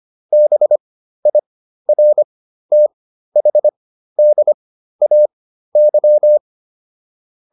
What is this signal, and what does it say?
Morse code, heard as a keyed tone, about 25 wpm, spaced as for 13 wpm, 600 Hz. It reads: BIRTHDAY